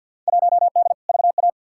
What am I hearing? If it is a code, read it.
Morse code, heard as a keyed tone, about 52 wpm, 700 Hz. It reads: 1D VA
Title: Field Day Exchanges - 50wpm